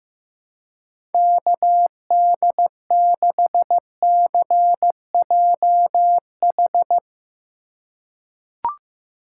Morse code, heard as a keyed tone, about 16 wpm, 700 Hz. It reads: KD6CJH
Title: USA Call Signs - 6 Characters - 4x Character Spacing - 15wpm